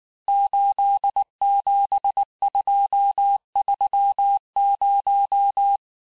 Morse code, heard as a keyed tone, about 19 wpm, 800 Hz. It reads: 87230
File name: cours-du-lundi files / cours-1595-teste-CHIFFRE-vitesse18-MPM